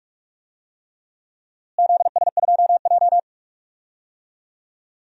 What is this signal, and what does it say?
Morse code, heard as a keyed tone, about 41 wpm, 700 Hz. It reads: ZS2J
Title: DXCC-100 Call Signs - 4 Characters - Repeated 3x - 45wpm